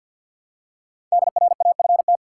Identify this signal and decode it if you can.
Morse code, heard as a keyed tone, about 47 wpm, 700 Hz. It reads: DRAFT
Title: Five Letter Words - 50wpm